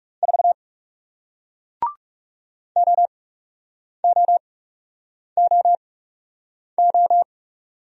Morse code, heard as a keyed tone, about 40 wpm, spaced as for 8 wpm, 700 Hz. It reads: 4 OOOO